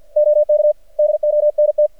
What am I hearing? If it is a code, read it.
Morse code, heard as a keyed tone, about 49 wpm, 600 Hz. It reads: OK DONT
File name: archived / CRSnet